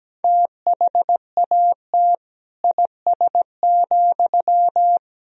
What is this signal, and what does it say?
Morse code, heard as a keyed tone, about 17 wpm, 700 Hz. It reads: THAT IS,
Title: Sentences from Top 1000 Words - Review Part 4 - 17wpm